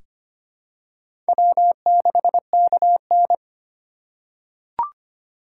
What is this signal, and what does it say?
Morse code, heard as a keyed tone, about 25 wpm, 700 Hz. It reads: W6KN
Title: USA Call Signs - 4 Characters - 8x Character Spacing - 25wpm